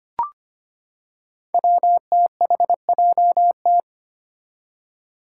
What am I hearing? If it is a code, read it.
Morse code, heard as a keyed tone, about 25 wpm, 700 Hz. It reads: WTHJT
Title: Five Challenging Letter-Number Combinations 25wpm